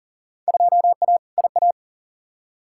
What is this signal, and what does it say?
Morse code, heard as a keyed tone, about 40 wpm, 700 Hz. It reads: 2A IA
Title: Field Day Exchanges Encore - 40wpm